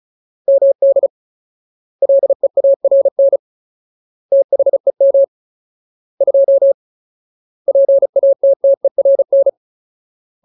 Morse code, heard as a keyed tone, about 37 wpm, 550 Hz. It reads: MD LEARN THEM 2 PATTERN
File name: Ultimate Word List / 35WPM Ultimate